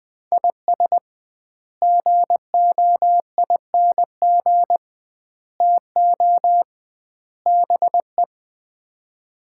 Morse code, heard as a keyed tone, about 21 wpm, 700 Hz. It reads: IS GOING TO BE
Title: 2x Word-Spacing - Sets of 4 Words - 20wpm